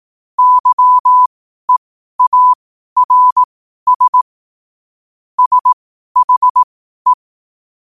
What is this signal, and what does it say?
Morse code, heard as a keyed tone, about 18 wpm, spaced as for 12 wpm, 1000 Hz. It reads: YEARS SHE
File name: qotdfiles / 12wpm